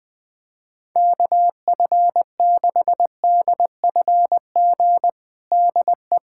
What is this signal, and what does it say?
Morse code, heard as a keyed tone, about 21 wpm, 700 Hz. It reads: KF6DFG DE
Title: USA Callsign DE Callsign - 6 Characters - 20wpm